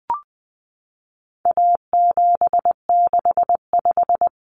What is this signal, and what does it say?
Morse code, heard as a keyed tone, about 20 wpm, 700 Hz. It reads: A765
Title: Common Cut Numbers - 4-Digit 20wpm